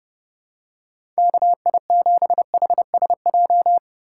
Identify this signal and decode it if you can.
Morse code, heard as a keyed tone, about 30 wpm, 700 Hz. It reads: KI7HSJ
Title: USA Call Signs - 4x Character Spacing - 30wpm